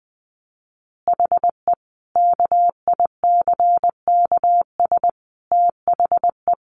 Morse code, heard as a keyed tone, about 20 wpm, 700 Hz. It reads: HE KICKS THE
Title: Additional Easy Phrases 20wpm